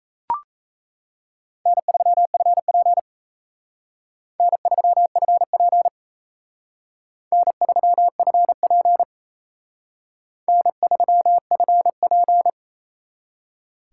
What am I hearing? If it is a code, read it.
Morse code, heard as a keyed tone, about 35 wpm, 700 Hz. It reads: N3FP N3FP N3FP N3FP
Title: Speed Racing - USA 1x2 Call Signs 28wpm